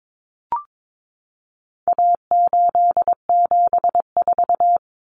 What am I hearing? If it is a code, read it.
Morse code, heard as a keyed tone, about 22 wpm, 700 Hz. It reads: A874
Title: Common Cut Numbers - 4-Digit 22wpm